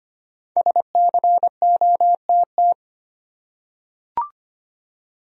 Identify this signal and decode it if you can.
Morse code, heard as a keyed tone, about 25 wpm, 700 Hz. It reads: SCOTT
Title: USA Common Names - 25wpm